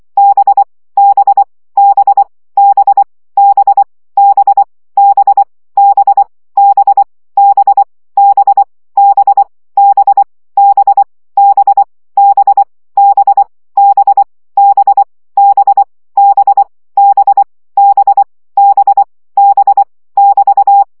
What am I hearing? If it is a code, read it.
Morse code, heard as a keyed tone, about 24 wpm, 800 Hz. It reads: B B B B B B B B B B B B B B B B B B B B B B B B B <BT>
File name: CW gyakorlas b 20 baud